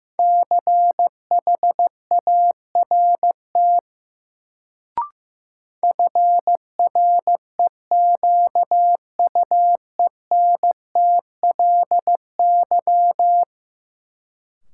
Morse code, heard as a keyed tone, about 15 wpm, 700 Hz. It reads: CHART FREQUENTLY
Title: Sentences from Top 1300 Words - New Word Drill - 15wpm